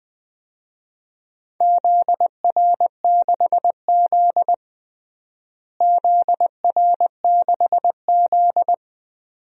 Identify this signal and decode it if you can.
Morse code, heard as a keyed tone, about 20 wpm, 700 Hz. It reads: ZR6Z ZR6Z
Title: DXCC-100 Call Signs - 4 Characters - Repeated 2x - 20wpm